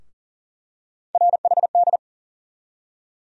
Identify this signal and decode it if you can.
Morse code, heard as a keyed tone, about 40 wpm, 700 Hz. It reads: RHD
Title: USA Call Sign Postfixes - 3 Letters 40wpm